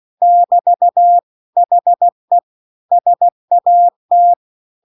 Morse code, heard as a keyed tone, about 16 wpm, 700 Hz. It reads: <BT> HE SAT